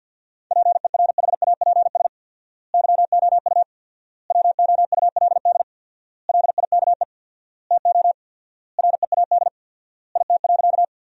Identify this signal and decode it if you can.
Morse code, heard as a keyed tone, about 51 wpm, 700 Hz. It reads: PERHAPS YOU WOULD LIKE TO READ IT.
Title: 4x Word-Spacing - Sentences from Top 500 Words - 50wpm